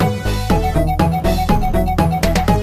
There is no code.